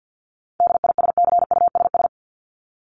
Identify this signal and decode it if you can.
Morse code, heard as a keyed tone, about 50 wpm, 700 Hz. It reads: DISCUSS
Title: Top 751-1000 Words 50wpm